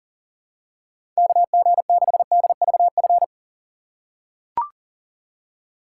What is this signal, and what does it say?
Morse code, heard as a keyed tone, about 40 wpm, 700 Hz. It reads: KG6DVF
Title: USA Call Signs - 6 Characters - 3x Character Spacing - 40wpm